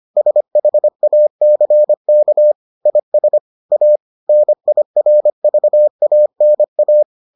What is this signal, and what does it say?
Morse code, heard as a keyed tone, about 25 wpm, 600 Hz. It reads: SHACK IS A NIRVANA